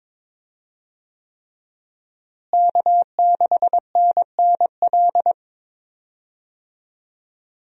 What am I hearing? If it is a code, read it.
Morse code, heard as a keyed tone, about 22 wpm, 700 Hz. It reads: K6NNL
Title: USA Call Signs - 5 Characters - Repeated 3x - 22wpm